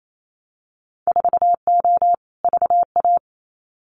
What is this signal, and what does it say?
Morse code, heard as a keyed tone, about 28 wpm, 700 Hz. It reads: 4O VA
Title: Winter Field Day Exchanges Encore 28wpm